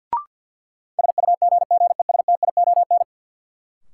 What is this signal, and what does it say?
Morse code, heard as a keyed tone, about 47 wpm, 700 Hz. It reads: SUGGESTION
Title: Sentences from Top 1600 Words - New Word Drill Alt - 50wpm